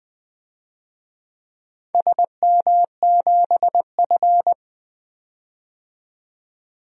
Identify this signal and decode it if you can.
Morse code, heard as a keyed tone, about 20 wpm, 700 Hz. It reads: SM7F
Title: DXCC-100 Call Signs - 4 Characters - 20wpm